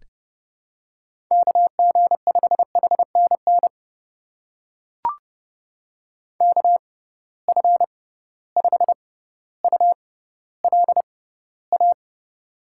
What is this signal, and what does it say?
Morse code, heard as a keyed tone, about 30 wpm, spaced as for 15 wpm, 700 Hz. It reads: KG5HNN KF5ULA
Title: USA Call Signs - 6x Character Spacing - 30wpm